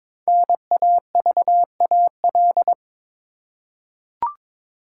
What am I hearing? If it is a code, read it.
Morse code, heard as a keyed tone, about 22 wpm, 700 Hz. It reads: NAVAL
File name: 5 Letter Words - First 3 Letters Given - 22wpm